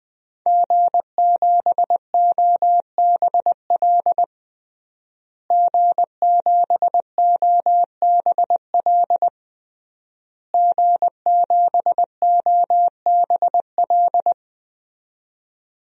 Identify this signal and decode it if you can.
Morse code, heard as a keyed tone, about 20 wpm, 700 Hz. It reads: G7OBL G7OBL G7OBL
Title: DXCC-100 Call Signs - 5 Characters - Repeated 3x - 20wpm